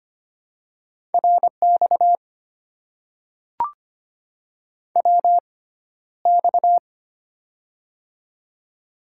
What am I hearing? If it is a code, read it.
Morse code, heard as a keyed tone, about 25 wpm, spaced as for 13 wpm, 700 Hz. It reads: RX WX
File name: Top 90 QSO Elements - 6x Character Spacing - 25wpm